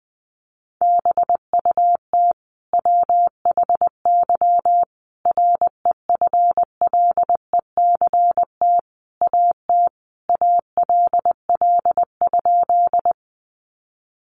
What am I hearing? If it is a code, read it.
Morse code, heard as a keyed tone, about 20 wpm, 700 Hz. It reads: BUT WHY REFLECT AT ALL?
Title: Sentences from Top 1500 Words 20wpm